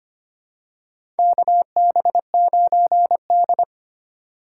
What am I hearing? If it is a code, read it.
Morse code, heard as a keyed tone, about 25 wpm, 700 Hz. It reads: KB9D
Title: USA Call Signs - 4 Characters - 2x Character Spacing - 25wpm